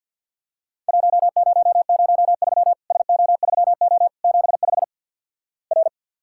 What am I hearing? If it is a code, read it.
Morse code, heard as a keyed tone, about 54 wpm, 700 Hz. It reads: 1003 SO3O 75 R
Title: Worked All Europe DX Contest - QTC Exchange - 50wpm